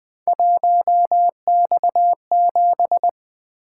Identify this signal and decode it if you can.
Morse code, heard as a keyed tone, about 20 wpm, 700 Hz. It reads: 1X7
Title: Three Letter-Number Combinations - 20wpm